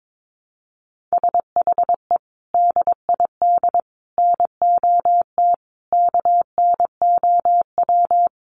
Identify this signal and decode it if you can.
Morse code, heard as a keyed tone, about 22 wpm, 700 Hz. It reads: SHE DID NOT KNOW
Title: Sentences from Top 1100 Words 22wpm